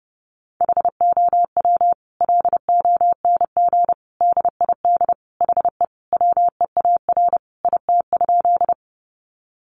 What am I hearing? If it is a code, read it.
Morse code, heard as a keyed tone, about 30 wpm, 700 Hz. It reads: HOW LONG DID HE WEAR IT?
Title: Sentences from Top 1500 Words 30wpm